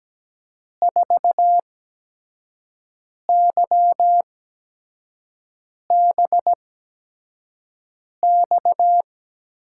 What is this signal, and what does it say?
Morse code, heard as a keyed tone, about 17 wpm, spaced as for 5 wpm, 700 Hz. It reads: 4YBX
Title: USA Call Signs - 6 Characters - 8x Character Spacing - 17wpm